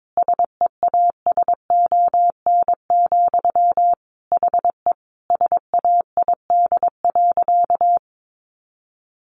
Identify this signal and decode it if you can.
Morse code, heard as a keyed tone, about 22 wpm, 700 Hz. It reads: SEASON, HE SAID.
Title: Sentences from Top 1400 Words 22wpm